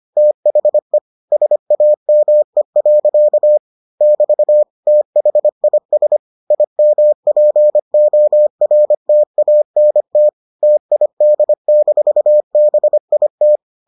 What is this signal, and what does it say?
Morse code, heard as a keyed tone, about 25 wpm, 600 Hz. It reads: THE SAME. <BT> THIS IMPORTANT TID-BIT